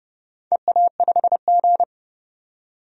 Morse code, heard as a keyed tone, about 30 wpm, 700 Hz. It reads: EA5G
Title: DXCC-100 Call Signs - 4 Characters - 30wpm